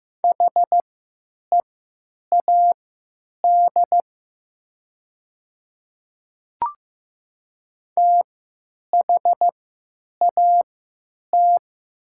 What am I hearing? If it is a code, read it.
Morse code, heard as a keyed tone, about 15 wpm, spaced as for 9 wpm, 700 Hz. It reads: HEAD THAT
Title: 3x Character Spacing - Top 300 Words - 15wpm